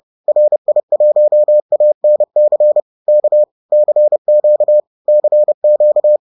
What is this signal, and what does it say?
Morse code, heard as a keyed tone, about 30 wpm, 600 Hz. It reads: RI1ANC K CQ CQ